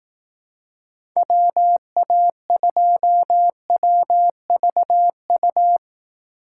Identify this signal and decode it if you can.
Morse code, heard as a keyed tone, about 18 wpm, 700 Hz. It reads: WA2WVU
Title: Speed Racing - USA Call Signs 15wpm